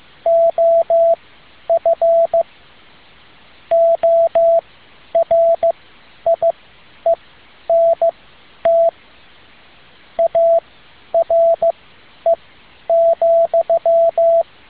Morse code, heard as a keyed tone, about 15 wpm, spaced as for 10 wpm, 670 Hz, with radio noise. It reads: OF ORIENT ARE,